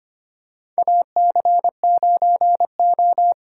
Morse code, heard as a keyed tone, about 25 wpm, 700 Hz. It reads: AC9O
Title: USA Call Signs - 8x Character Spacing - 25wpm